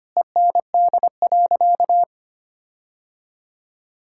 Morse code, heard as a keyed tone, about 25 wpm, 700 Hz. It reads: END.
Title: Sentences from Top 1600 Words 25wpm